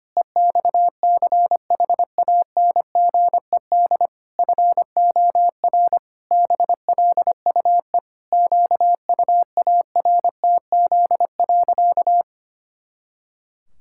Morse code, heard as a keyed tone, about 25 wpm, 700 Hz. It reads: EXCHANGED FOR BLUE QUARTZ.
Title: Pangram Challenge v2 - 25wpm